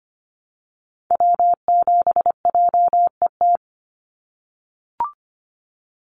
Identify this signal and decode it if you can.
Morse code, heard as a keyed tone, about 25 wpm, 700 Hz. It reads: W7JET
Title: NA SOTA Activator Callsigns 25wpm